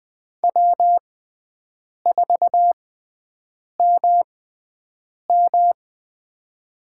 Morse code, heard as a keyed tone, about 21 wpm, spaced as for 7 wpm, 700 Hz. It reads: W4MM
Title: USA Call Signs - 4 Characters - 6x Character Spacing - 20wpm